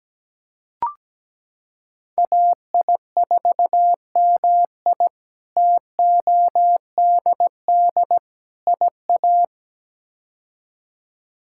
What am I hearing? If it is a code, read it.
Morse code, heard as a keyed tone, about 17 wpm, 700 Hz. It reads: AI4MI TODD IA
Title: K1USN SST QSO Practice - 17wpm